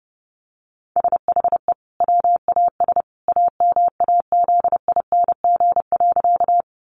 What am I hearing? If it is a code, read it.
Morse code, heard as a keyed tone, about 30 wpm, 700 Hz. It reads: SHE WAS AMAZING.
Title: Sentences from Top 1500 Words 30wpm